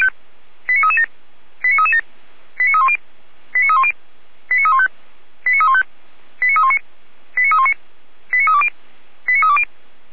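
Five-tone selective call (ZVEI), heard as quick runs of five tones, tones of about 80 ms. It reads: 89337 89337 89310 89310 89315 89315 89319 89319 89320 89320
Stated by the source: ZVEI five-tone call 89304 received on SDR.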